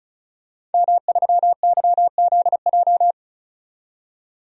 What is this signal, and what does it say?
Morse code, heard as a keyed tone, about 36 wpm, 700 Hz. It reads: M3YZJ
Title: DXCC-100 Call Signs - 5 Characters - 35wpm